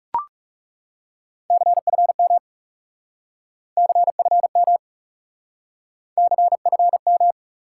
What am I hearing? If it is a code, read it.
Morse code, heard as a keyed tone, about 39 wpm, 700 Hz. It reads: CFM CFM CFM